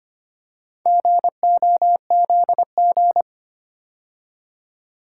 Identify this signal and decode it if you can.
Morse code, heard as a keyed tone, about 25 wpm, 700 Hz. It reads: GOZG
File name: practice / 25wpm / Four Letter-Number Combinations - 25wpm